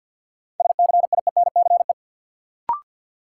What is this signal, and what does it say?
Morse code, heard as a keyed tone, about 51 wpm, 700 Hz. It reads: SCIENCE
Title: Sentences from Top 1000 Words - Review Part 2 - New Word Drill - 50wpm